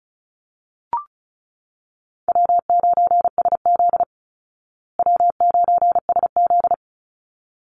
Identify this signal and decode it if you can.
Morse code, heard as a keyed tone, about 34 wpm, 700 Hz. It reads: W9SZ W9SZ
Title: Call Signs Repeated 3x - 35wpm